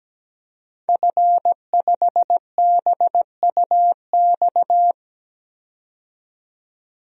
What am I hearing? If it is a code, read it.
Morse code, heard as a keyed tone, about 17 wpm, 700 Hz. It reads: F5BUX
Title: DXCC-100 Call Signs - 5 Characters - 17wpm